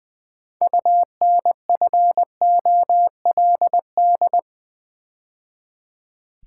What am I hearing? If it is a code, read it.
Morse code, heard as a keyed tone, about 20 wpm, 700 Hz. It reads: UNFOLD
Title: Build A Word - 2 Letters At A Time - 6-Letter Words - 20wpm